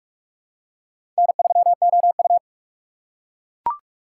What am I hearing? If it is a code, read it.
Morse code, heard as a keyed tone, about 44 wpm, 700 Hz. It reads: N3OU
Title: USA Call Signs - 4 Characters - 6x Character Spacing - 45wpm